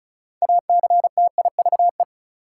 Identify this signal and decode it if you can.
Morse code, heard as a keyed tone, about 36 wpm, 700 Hz. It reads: ACTIVE
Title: Sentences from Top 900 Words - New Word Drill - 35wpm